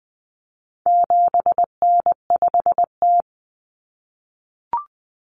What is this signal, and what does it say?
Morse code, heard as a keyed tone, about 20 wpm, 700 Hz. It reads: 7N5T
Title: Common Cut Numbers - 4-Digit 20wpm